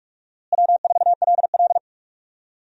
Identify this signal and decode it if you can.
Morse code, heard as a keyed tone, about 46 wpm, 700 Hz. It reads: W4LL
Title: USA Call Signs - 4 Characters - Repeated 3x - 45wpm